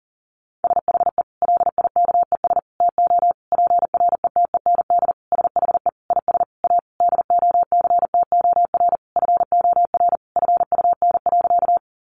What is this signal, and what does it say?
Morse code, heard as a keyed tone, about 40 wpm, 700 Hz. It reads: SHE LIKES TO PRETEND SHE IS A DOCTOR FOR FUN.
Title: Additional Easy Phrases 40wpm